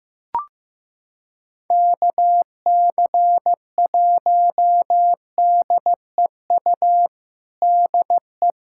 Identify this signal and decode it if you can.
Morse code, heard as a keyed tone, about 15 wpm, 700 Hz. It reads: KC1DEU DE
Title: USA Callsign DE Callsign - 6 Characters - 15wpm